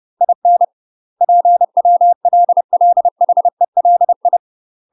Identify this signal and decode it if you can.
Morse code, heard as a keyed tone, about 30 wpm, 700 Hz. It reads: IN PWLLHELI
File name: UK MASTER QSOs TRAINING / 30WPM / UK MASTER QSOs 0046